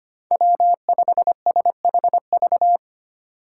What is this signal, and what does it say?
Morse code, heard as a keyed tone, about 25 wpm, 700 Hz. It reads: W5SHV
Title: USA Call Signs - 5 Characters - 8x Character Spacing - 25wpm